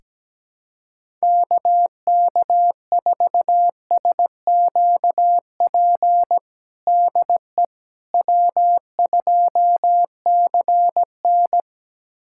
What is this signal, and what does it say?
Morse code, heard as a keyed tone, about 17 wpm, 700 Hz. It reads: KK4SQP DE W2CN
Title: USA Callsign DE Callsign 17wpm